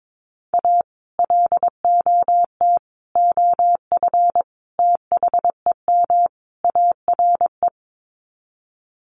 Morse code, heard as a keyed tone, about 22 wpm, 700 Hz. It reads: A LOT OF THEM ARE
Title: Sets of 5 Words Encore2 22wpm